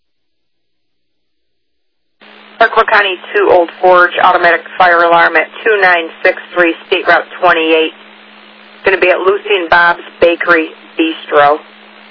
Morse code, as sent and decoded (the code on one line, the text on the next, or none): none